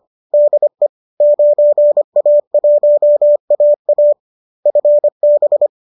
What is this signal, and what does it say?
Morse code, heard as a keyed tone, about 25 wpm, 590 Hz. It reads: DE 9A1AA FB
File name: qso219 25